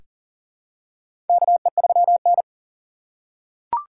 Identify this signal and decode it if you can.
Morse code, heard as a keyed tone, about 40 wpm, 690 Hz. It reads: KE3N
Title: USA Call Signs - 4 Characters - 2x Character Spacing - 40wpm